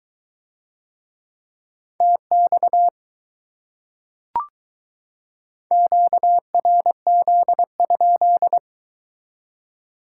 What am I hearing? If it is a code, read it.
Morse code, heard as a keyed tone, about 23 wpm, 700 Hz. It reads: TX QRZ?